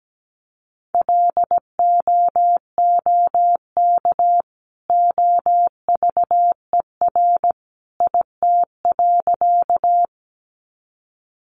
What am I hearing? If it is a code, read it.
Morse code, heard as a keyed tone, about 17 wpm, 700 Hz. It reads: LOOK OVER IT.
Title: Sentences from Top 100 Words 17wpm